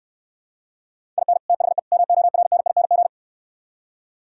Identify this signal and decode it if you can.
Morse code, heard as a keyed tone, about 66 wpm, 700 Hz. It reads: IN THE MORNING